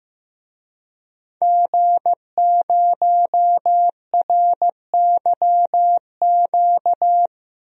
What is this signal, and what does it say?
Morse code, heard as a keyed tone, about 15 wpm, 700 Hz. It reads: G0RYQ
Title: DXCC-100 Call Signs - 5 Characters - Repeated 3x - 15wpm